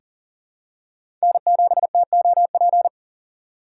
Morse code, heard as a keyed tone, about 40 wpm, 700 Hz. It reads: N7TOP
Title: USA Call Signs - 5 Characters - 2x Character Spacing - 40wpm